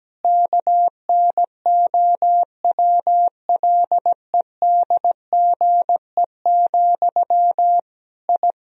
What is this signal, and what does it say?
Morse code, heard as a keyed tone, about 17 wpm, 700 Hz. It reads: KNOWLEDGE, I
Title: Sentences from Top 1000 Words - Review Part 3 - 17wpm